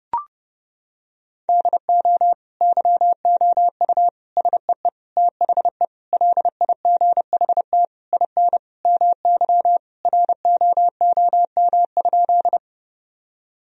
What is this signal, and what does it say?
Morse code, heard as a keyed tone, about 30 wpm, 700 Hz. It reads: DO YOU SEE THE LIGHT IN MY ROOM?
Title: Sentences from Top 1000 Words - Review Part 1 - 30wpm